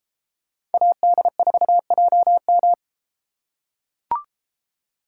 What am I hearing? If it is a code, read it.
Morse code, heard as a keyed tone, about 31 wpm, 700 Hz. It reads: AD4JM